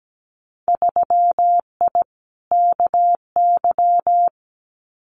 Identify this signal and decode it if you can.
Morse code, heard as a keyed tone, about 17 wpm, 700 Hz. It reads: 3I KY
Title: Winter Field Day Exchanges 17wpm